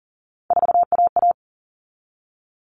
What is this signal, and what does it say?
Morse code, heard as a keyed tone, about 40 wpm, 700 Hz. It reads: 4AA
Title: All Cut Numbers - 3-Digit 40wpm